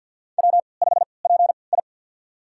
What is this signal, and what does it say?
Morse code, heard as a keyed tone, about 49 wpm, spaced as for 28 wpm, 700 Hz. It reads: W5PI
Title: USA Call Signs - 4 Characters - 3x Character Spacing - 50wpm